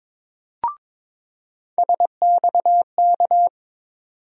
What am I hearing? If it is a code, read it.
Morse code, heard as a keyed tone, about 22 wpm, 700 Hz. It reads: SXK
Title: USA Call Sign Postfixes - 3 Letters 22wpm